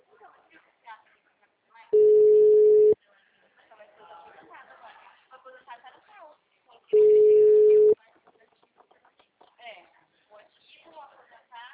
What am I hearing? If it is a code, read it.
Ringback tone (Europe, 425 Hz), with speech.